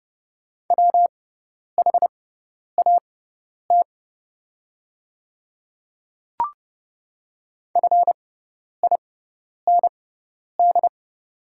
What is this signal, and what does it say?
Morse code, heard as a keyed tone, about 30 wpm, spaced as for 10 wpm, 700 Hz. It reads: WHAT FIND
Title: 6x Character Spacing - Top 100 Words - 30wpm